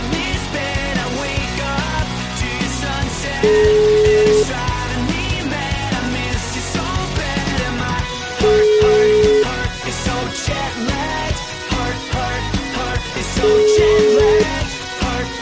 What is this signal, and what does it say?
Ringback tone (Europe, 425 Hz), with noise.